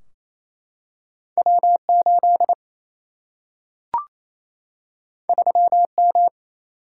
Morse code, heard as a keyed tone, about 28 wpm, 700 Hz. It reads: W8 3M
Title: Single-Letter Single-Number 28wpm